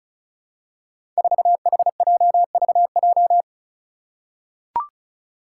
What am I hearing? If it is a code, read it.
Morse code, heard as a keyed tone, about 36 wpm, 700 Hz. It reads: 4HJVJ
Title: Five Challenging Letter-Number Combinations 35wpm